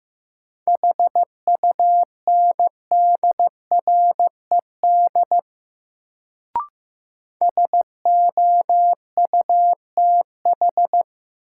Morse code, heard as a keyed tone, about 15 wpm, 700 Hz. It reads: HUNDRED SOUTH
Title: Sentences from Top 400 Words - New Word Drill - 15wpm